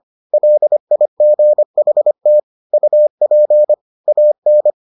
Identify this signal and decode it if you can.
Morse code, heard as a keyed tone, about 25 wpm, 600 Hz. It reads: LIGHT UP AN